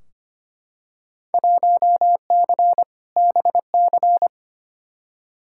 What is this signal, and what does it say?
Morse code, heard as a keyed tone, about 25 wpm, 700 Hz. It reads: 1C BC
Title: Field Day Exchanges Encore - 25wpm